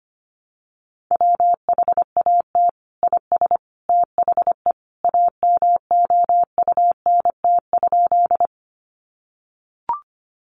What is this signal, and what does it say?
Morse code, heard as a keyed tone, about 25 wpm, 700 Hz. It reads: WHAT IS THE AMOUNT?
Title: Sentences from Top 700 Words 25wpm